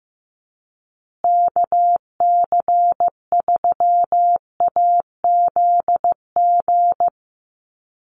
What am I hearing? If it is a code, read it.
Morse code, heard as a keyed tone, about 15 wpm, 700 Hz. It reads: KC3AZG
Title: USA Call Signs Repeated 2x 15wpm